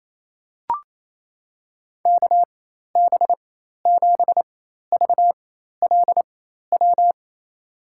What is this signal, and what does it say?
Morse code, heard as a keyed tone, about 28 wpm, spaced as for 13 wpm, 690 Hz. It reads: KB7VLW
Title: USA Call Signs - 6 Characters - 4x Character Spacing - 28wpm